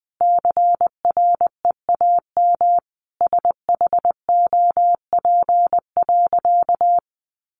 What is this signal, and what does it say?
Morse code, heard as a keyed tone, about 20 wpm, 700 Hz. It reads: CREAM SHOP.
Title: Additional Easy Phrases 20wpm